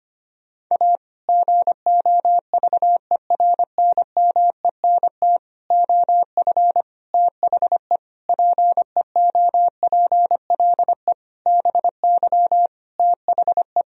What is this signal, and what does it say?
Morse code, heard as a keyed tone, about 25 wpm, 700 Hz. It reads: A GOVERNMENT OF THE PEOPLE BY THE
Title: Sentences from Top 1000 Words - Review Part 3 - 25wpm